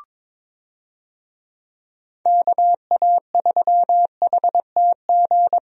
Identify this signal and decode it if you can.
Morse code, heard as a keyed tone, about 22 wpm, 700 Hz. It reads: KA3HTG
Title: USA Call Signs - 6 Characters - Repeated 3x - 22wpm